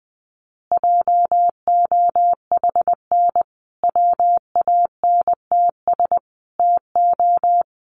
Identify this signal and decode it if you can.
Morse code, heard as a keyed tone, about 20 wpm, 700 Hz. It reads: JOHN WANTS TO
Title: Additional Easy Phrases 20wpm